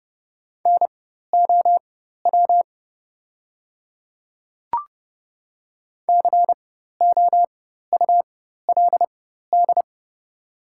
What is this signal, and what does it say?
Morse code, heard as a keyed tone, about 30 wpm, spaced as for 14 wpm, 690 Hz. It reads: NOW COULD
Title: 4x Character Spacing - Top 100 Words - 30wpm